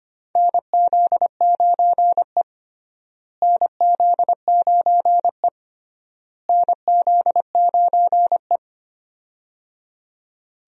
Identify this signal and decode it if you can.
Morse code, heard as a keyed tone, about 25 wpm, 700 Hz. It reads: NZ9E NZ9E NZ9E
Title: USA Call Signs - 4 Characters - Repeated 3x - 25wpm